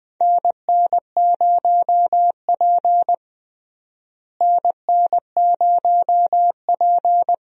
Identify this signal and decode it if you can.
Morse code, heard as a keyed tone, about 21 wpm, 700 Hz. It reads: NN0P NN0P
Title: USA Call Signs - 4 Characters - Repeated 3x - 20wpm